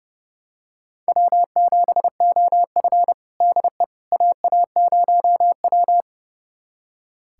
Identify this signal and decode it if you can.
Morse code, heard as a keyed tone, about 30 wpm, 700 Hz. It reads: W7OF DE AA0W
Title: USA Callsign DE Callsign - 4 Characters - 30wpm